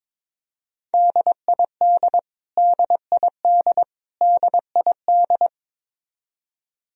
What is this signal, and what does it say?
Morse code, heard as a keyed tone, about 22 wpm, 700 Hz. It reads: DID DID DID
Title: Top 100 Words Repeated 22wpm